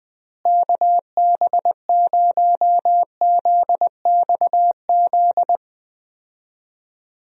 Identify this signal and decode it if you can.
Morse code, heard as a keyed tone, about 20 wpm, 700 Hz. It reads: KB0ZXZ
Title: USA Call Signs - 6 Characters - Repeated 3x - 20wpm